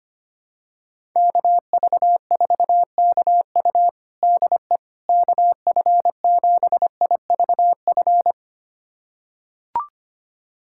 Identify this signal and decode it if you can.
Morse code, heard as a keyed tone, about 25 wpm, 700 Hz. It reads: KV4KU DE KF7IVF
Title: USA Callsign DE Callsign 25wpm